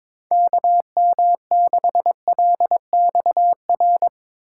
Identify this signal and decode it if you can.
Morse code, heard as a keyed tone, about 23 wpm, 700 Hz. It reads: KM6LXR
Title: Speed Racing - USA 2x3 Call Signs 22wpm